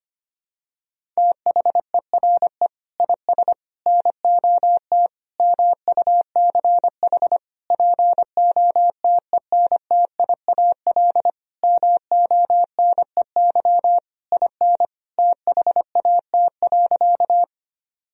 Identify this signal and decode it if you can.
Morse code, heard as a keyed tone, about 25 wpm, 700 Hz. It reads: THERE IS NOT MUCH POTENTIAL MONEY IN THAT.
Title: Sentences from Top 1000 Words - Review Part 4 - 25wpm